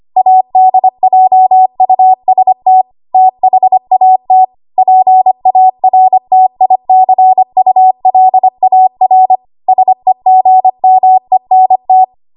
Morse code, heard as a keyed tone, about 25 wpm, 750 Hz. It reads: ADJUST THAT PARTICULAR SEGMENT
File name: Morse / 25WPM